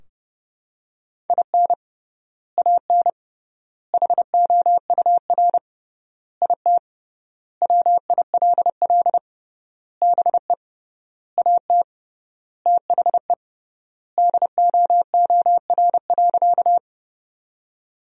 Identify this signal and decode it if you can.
Morse code, heard as a keyed tone, about 30 wpm, 700 Hz. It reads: IN AN HOUR IT WILL BE AT THE DOOR.
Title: 3x Word-Spacing - Sentences from Top 400 Words - 30wpm